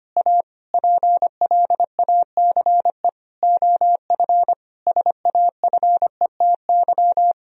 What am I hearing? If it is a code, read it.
Morse code, heard as a keyed tone, about 25 wpm, 700 Hz. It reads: A PLACE OF SAFETY
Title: Sentences from Top 1000 Words - Review Part 4 - 25wpm